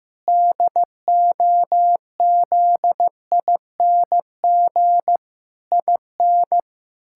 Morse code, heard as a keyed tone, about 15 wpm, 700 Hz. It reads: DOZING IN
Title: Pangram Challenge v2 - 15wpm